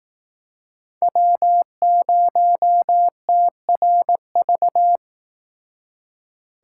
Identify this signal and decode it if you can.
Morse code, heard as a keyed tone, about 19 wpm, 700 Hz. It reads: W0TRV
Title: Speed Racing - USA 1x3 Call Signs 15wpm